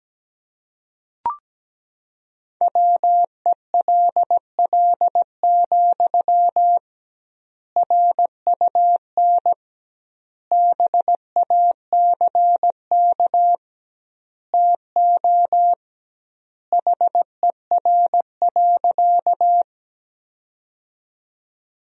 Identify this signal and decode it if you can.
Morse code, heard as a keyed tone, about 17 wpm, 700 Hz. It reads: WELL, RUN BACK TO HER.
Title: 2x Word-Spacing - Sentences from Top 300 Words - 17wpm